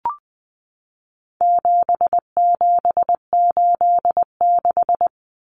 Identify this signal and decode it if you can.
Morse code, heard as a keyed tone, about 20 wpm, 700 Hz. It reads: 7786
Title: Numbers - 4-Digit 20wpm